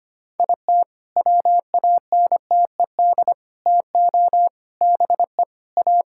Morse code, heard as a keyed tone, about 25 wpm, 700 Hz. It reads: IT WANTED TO BE A
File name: practice / 25wpm / All-Age Appropriate Jokes - 25wpm